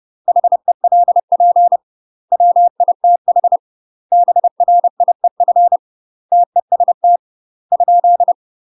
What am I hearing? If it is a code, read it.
Morse code, heard as a keyed tone, about 30 wpm, 700 Hz. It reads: HELP WITH BRIEF TEST ?
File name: UK MASTER QSOs TRAINING / 30WPM / UK MASTER QSOs 0023